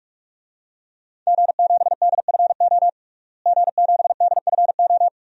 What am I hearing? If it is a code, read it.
Morse code, heard as a keyed tone, about 43 wpm, 700 Hz. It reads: G7DFO G7DFO
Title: DXCC-100 Call Signs - 5 Characters - Repeated 2x - 45wpm